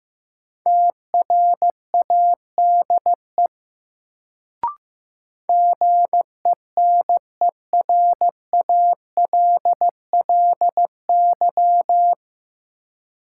Morse code, heard as a keyed tone, about 16 wpm, 700 Hz. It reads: TRADE GENERALLY
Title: Sentences from Top 600 Words - New Word Drill - 15wpm